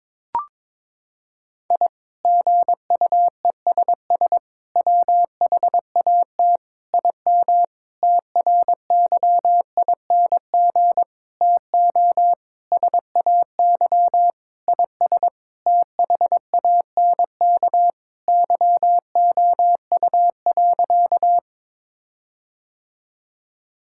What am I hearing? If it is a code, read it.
Morse code, heard as a keyed tone, about 22 wpm, 700 Hz. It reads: I GUESS WHAT IM TRYING TO SAY IS THANK YOU.
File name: practice / 22wpm / Sentences with 8-Word Phrases - 22wpm